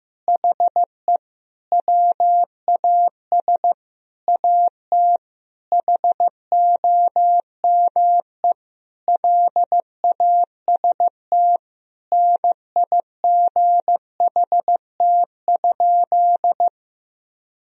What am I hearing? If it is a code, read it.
Morse code, heard as a keyed tone, about 15 wpm, 700 Hz. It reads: HE WAS AT HOME LAST NIGHT?
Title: Sentences from Top 1000 Words - Review Part 2 - 15wpm